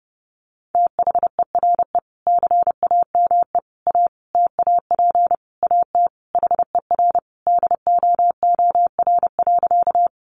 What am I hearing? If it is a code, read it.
Morse code, heard as a keyed tone, about 30 wpm, 700 Hz. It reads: THERE CAME A TAP AT HER DOOR.
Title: Sentences from Top 1400 Words 30wpm